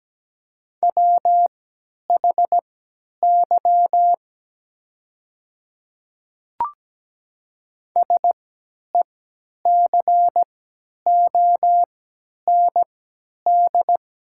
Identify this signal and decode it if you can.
Morse code, heard as a keyed tone, about 17 wpm, spaced as for 10 wpm, 700 Hz. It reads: WHY SECOND
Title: 3x Character Spacing - Top 500 Words - 17wpm